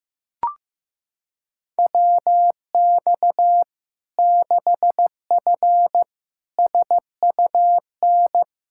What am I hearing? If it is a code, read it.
Morse code, heard as a keyed tone, about 15 wpm, 700 Hz. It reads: WX 6F SUN
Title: Practice QSO - Exchange 2 - Info - 15wpm